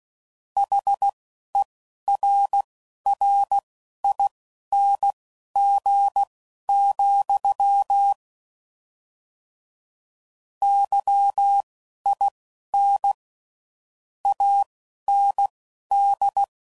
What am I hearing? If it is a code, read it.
Morse code, heard as a keyed tone, about 16 wpm, spaced as for 12 wpm, 800 Hz. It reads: HERRING, YIN AND
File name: Mixed / Misc / WordPairs